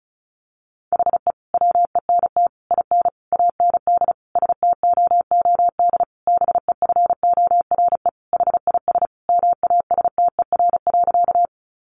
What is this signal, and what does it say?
Morse code, heard as a keyed tone, about 34 wpm, 700 Hz. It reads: HE WENT IN AND STOOD BEFORE HIS MASTER.
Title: Sentences from Top 1300 Words 35wpm